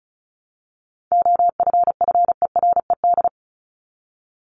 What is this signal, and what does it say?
Morse code, heard as a keyed tone, about 34 wpm, 700 Hz. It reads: OFFERED
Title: ED Suffix - Top 100 Words with Base Words 35wpm